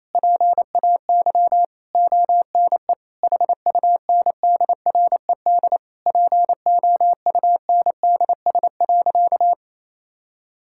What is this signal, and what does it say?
Morse code, heard as a keyed tone, about 28 wpm, 700 Hz. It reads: PAY ONE HUNDRED POUNDS.
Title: Sentences from Top 1000 Words - Review Part 6 - 28wpm